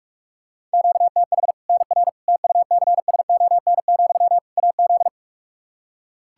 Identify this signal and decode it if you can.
Morse code, heard as a keyed tone, about 46 wpm, 700 Hz. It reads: QTH NR TUCSON, AZ
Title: K1USN SST QSO Practice 45wpm